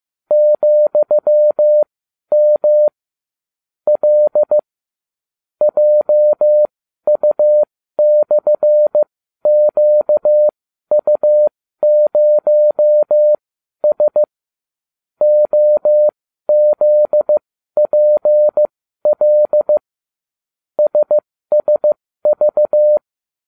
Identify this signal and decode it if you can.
Morse code, heard as a keyed tone, about 15 wpm, spaced as for 12 wpm, 600 Hz. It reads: ,M L JU/QU0S OZPL SSV